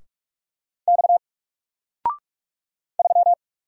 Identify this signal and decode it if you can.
Morse code, heard as a keyed tone, about 46 wpm, 700 Hz. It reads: X 3
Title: Single Letter-Number Alt 45wpm